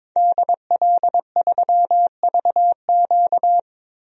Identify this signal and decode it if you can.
Morse code, heard as a keyed tone, about 21 wpm, 700 Hz. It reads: DL3VQ
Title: DXCC-100 Call Signs - 5 Characters - Repeated 2x - 22wpm